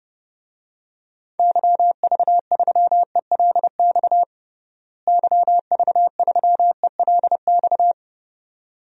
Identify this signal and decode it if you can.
Morse code, heard as a keyed tone, about 31 wpm, 700 Hz. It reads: YV3ELX YV3ELX
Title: DXCC-100 Call Signs - 6 Characters - Repeated 2x - 30wpm